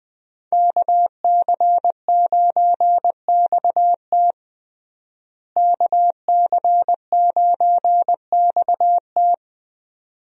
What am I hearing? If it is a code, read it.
Morse code, heard as a keyed tone, about 21 wpm, 700 Hz. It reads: KC9XT KC9XT
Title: USA Call Signs - 5 Characters - Repeated 2x - 20wpm